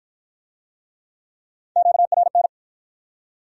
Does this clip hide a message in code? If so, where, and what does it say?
Yes: Morse code from 1.8 to 2.5 s, heard as a keyed tone, about 50 wpm, 700 Hz. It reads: QRN